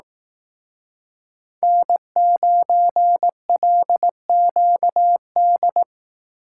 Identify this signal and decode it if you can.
Morse code, heard as a keyed tone, about 18 wpm, 700 Hz. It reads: N9LQD
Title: Speed Racing - USA 1x3 Call Signs 15wpm